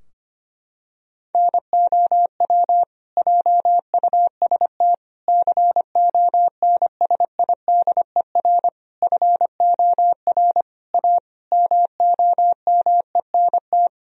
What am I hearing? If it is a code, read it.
Morse code, heard as a keyed tone, about 25 wpm, 700 Hz. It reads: NOW JUST CONSIDER FOR A MOMENT
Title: Sentences from Top 1000 Words - Review Part 4 - 25wpm